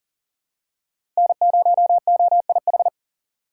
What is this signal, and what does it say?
Morse code, heard as a keyed tone, about 40 wpm, 700 Hz. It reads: N0OIH
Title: USA Call Signs - 5 Characters - 2x Character Spacing - 40wpm